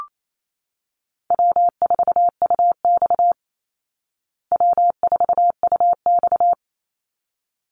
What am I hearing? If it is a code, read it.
Morse code, heard as a keyed tone, about 28 wpm, 700 Hz. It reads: W4UX W4UX
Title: Easy Call Signs Repeated 2x - 28wpm